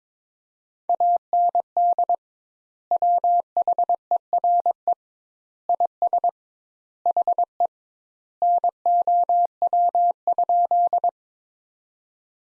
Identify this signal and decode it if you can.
Morse code, heard as a keyed tone, about 22 wpm, 700 Hz. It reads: AND WHERE IS HE NOW?
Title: Sentences Top 200 Words - 22wpm